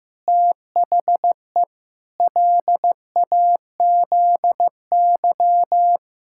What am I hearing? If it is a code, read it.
Morse code, heard as a keyed tone, about 15 wpm, 700 Hz. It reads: THE LAZY
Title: Pangram Challenge v2 - 15wpm